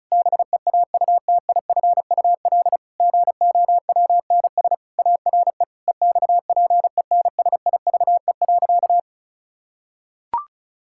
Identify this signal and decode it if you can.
Morse code, heard as a keyed tone, about 36 wpm, 700 Hz. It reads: BEAUTIFUL GOWNS ARE EXPENSIVE.
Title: Pangram Challenge v2 - 35wpm